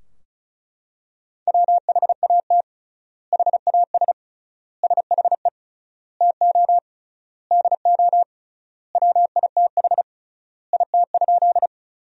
Morse code, heard as a keyed tone, about 34 wpm, 700 Hz. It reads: WHAT HAS SHE TO DO WITH IT?
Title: 3x Word-Spacing - Sentences from Top 100 Words - 35wpm